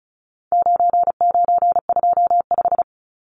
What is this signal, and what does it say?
Morse code, heard as a keyed tone, about 34 wpm, 700 Hz. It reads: 9925
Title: Numbers - 4-Digit 35wpm